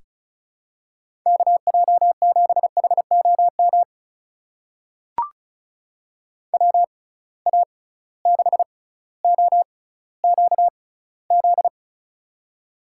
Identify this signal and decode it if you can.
Morse code, heard as a keyed tone, about 34 wpm, spaced as for 18 wpm, 700 Hz. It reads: KJ7HOM WA6OQZ
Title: USA Call Signs - 6x Character Spacing - 35wpm